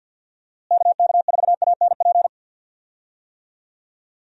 Morse code, heard as a keyed tone, about 47 wpm, 700 Hz. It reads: KK4ANP
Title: USA Call Signs - 6 Characters - Repeated 2x - 50wpm